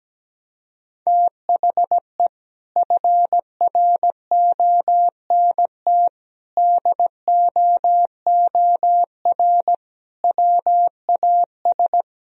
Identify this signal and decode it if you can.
Morse code, heard as a keyed tone, about 17 wpm, 700 Hz. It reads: THE FRONT DOOR WAS
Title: Sentences from Top 1000 Words - Review Part 2 - 17wpm